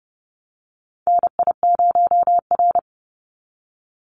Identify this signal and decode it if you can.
Morse code, heard as a keyed tone, about 30 wpm, 700 Hz. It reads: NI0R
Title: Call Signs - 30wpm